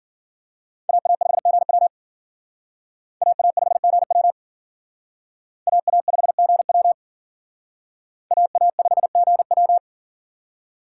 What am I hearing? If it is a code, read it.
Morse code, heard as a keyed tone, about 52 wpm, 700 Hz. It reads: AA5GW AA5GW AA5GW AA5GW